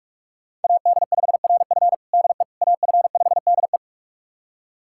Morse code, heard as a keyed tone, about 41 wpm, 700 Hz. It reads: AD5RF DE AF5DE
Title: USA Callsign DE Callsign - 5 Characters - 45wpm